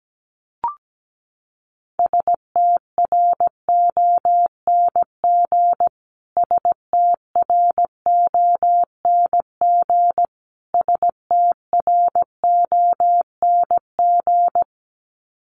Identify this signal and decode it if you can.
Morse code, heard as a keyed tone, about 17 wpm, 700 Hz. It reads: STRONG STRONG STRONG
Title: Top 500 Words Repeated 17wpm